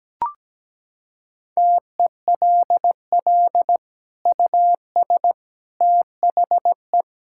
Morse code, heard as a keyed tone, about 17 wpm, 700 Hz. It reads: TELL US THE
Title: Sentences from Top 1600 Words 17wpm